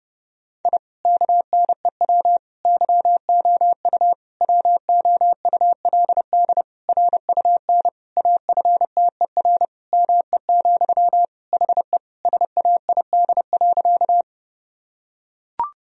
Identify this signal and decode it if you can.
Morse code, heard as a keyed tone, about 31 wpm, 700 Hz. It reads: I KNEW YOU WOULD RUN AFTER ME, HE SAID.
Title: Sentences from Top 1000 Words - Review Part 1 - 30wpm